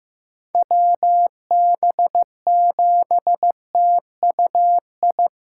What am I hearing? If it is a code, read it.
Morse code, heard as a keyed tone, about 15 wpm, 700 Hz. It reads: WB7TUI
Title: USA Callsign DE Callsign 15wpm